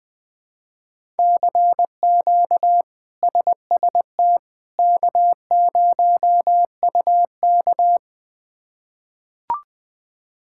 Morse code, heard as a keyed tone, about 20 wpm, 700 Hz. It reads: CQ SST K0UK
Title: K1USN SST QSO Practice - 20wpm